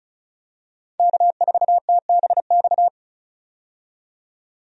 Morse code, heard as a keyed tone, about 34 wpm, 700 Hz. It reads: K4TBX
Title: USA Call Signs - 5 Characters - 35wpm